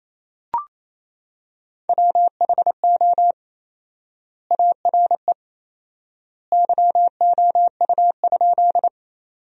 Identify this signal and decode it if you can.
Morse code, heard as a keyed tone, about 27 wpm, 700 Hz. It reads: WHO ARE YOU?
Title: 4x Word-Spacing - Sentences from Top 100 Words - 28wpm